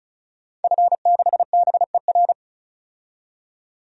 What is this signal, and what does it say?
Morse code, heard as a keyed tone, about 36 wpm, 700 Hz. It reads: F6BER
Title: DXCC-100 Call Signs - 5 Characters - 35wpm